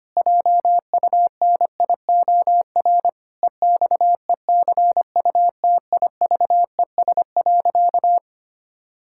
Morse code, heard as a keyed tone, about 25 wpm, 700 Hz. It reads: JUNIOR EXECUTIVES.
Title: Pangram Challenge v2 - 25wpm